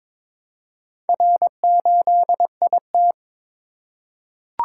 Morse code, heard as a keyed tone, about 22 wpm, 700 Hz. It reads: R8IT
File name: practice / 22wpm / Four Letter-Number Combinations - 22wpm